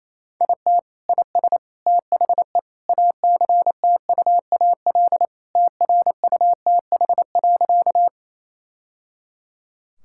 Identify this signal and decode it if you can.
Morse code, heard as a keyed tone, about 27 wpm, 700 Hz. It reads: IT IS THE ACTUAL TRUTH.
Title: Sentences from Top 1000 Words - Review Part 5 - 28wpm